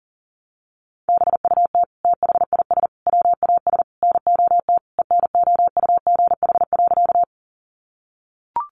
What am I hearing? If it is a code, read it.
Morse code, heard as a keyed tone, about 40 wpm, 700 Hz. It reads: BUT THIS WAS NOT ENOUGH.
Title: Sentences from Top 600 Words 40wpm